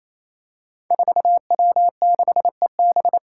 Morse code, heard as a keyed tone, about 27 wpm, 700 Hz. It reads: 4W6EB
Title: Five Challenging Letter-Number Combinations 28wpm